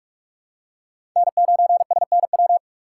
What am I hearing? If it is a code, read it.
Morse code, heard as a keyed tone, about 44 wpm, 700 Hz. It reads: N9SNW
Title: USA Call Signs - 5 Characters - 4x Character Spacing - 45wpm